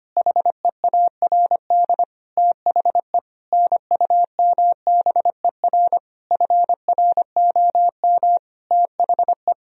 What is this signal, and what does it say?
Morse code, heard as a keyed tone, about 25 wpm, 700 Hz. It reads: HEARD THE NUMBER FROM THE
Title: Sentences from Top 1000 Words - Review Part 1 - 25wpm